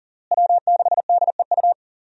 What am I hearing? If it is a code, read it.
Morse code, heard as a keyed tone, about 40 wpm, 700 Hz. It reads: W6DEU
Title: USA Call Signs - 5 Characters - Repeated 3x - 40wpm